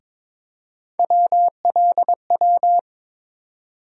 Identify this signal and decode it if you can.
Morse code, heard as a keyed tone, about 22 wpm, 700 Hz. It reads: WLW
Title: USA Call Sign Postfixes - 3 Letters 22wpm